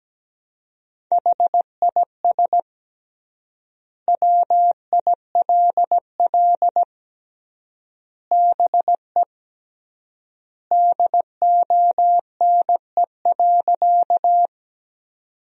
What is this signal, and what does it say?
Morse code, heard as a keyed tone, about 17 wpm, 700 Hz. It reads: HIS WILL BE DONE.
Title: 3x Word-Spacing - Sentences from Top 500 Words - 17wpm